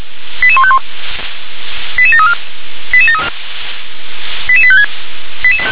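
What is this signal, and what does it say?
Five-tone selective call (ZVEI), heard as quick runs of five tones, tones of about 70 ms. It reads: 88151 88734 88856
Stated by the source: ZVEI five-tone selective call received with an SDR.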